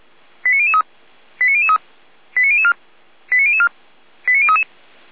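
Five-tone selective call (ZVEI), heard as quick runs of five tones, tones of about 70 ms. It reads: 89003 89003 89004 89004 89033
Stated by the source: ZVEI five-tone call 89099 received by SDR, from the server of Feuerwehr Orlamünde (fire brigade).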